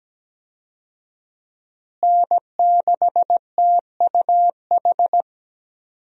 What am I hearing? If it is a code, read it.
Morse code, heard as a keyed tone, about 17 wpm, 700 Hz. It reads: N6TUH
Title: USA Call Signs - 5 Characters - Repeated 3x - 17wpm